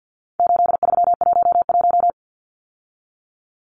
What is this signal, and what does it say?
Morse code, heard as a keyed tone, about 50 wpm, 700 Hz. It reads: 8311
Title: Numbers - 4-Digit 50wpm